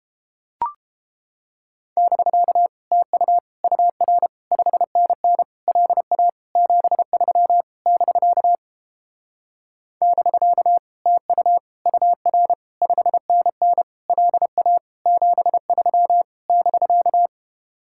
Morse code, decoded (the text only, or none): <BK> TU UR 5NN LA 73 <BK> <BK> TU UR 5NN LA 73 <BK>